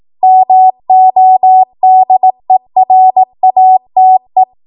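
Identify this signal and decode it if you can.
Morse code, heard as a keyed tone, about 18 wpm, 750 Hz. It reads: MODERATE
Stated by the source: Code practice at 18 WPM, a transition run bridging 15 to 20 WPM.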